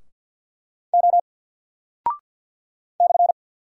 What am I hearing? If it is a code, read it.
Morse code, heard as a keyed tone, about 48 wpm, 700 Hz. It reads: O /
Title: Single Character (Letter:Number:Punctuation:Prosign) 50wpm